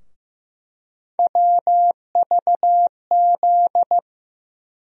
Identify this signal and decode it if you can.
Morse code, heard as a keyed tone, about 15 wpm, 700 Hz. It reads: WVZ
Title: USA Call Sign Postfixes - 3 Letters 15wpm